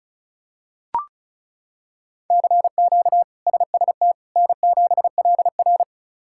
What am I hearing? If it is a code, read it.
Morse code, heard as a keyed tone, about 34 wpm, 700 Hz. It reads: CQ SST N7LR
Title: K1USN SST QSO Practice - 35wpm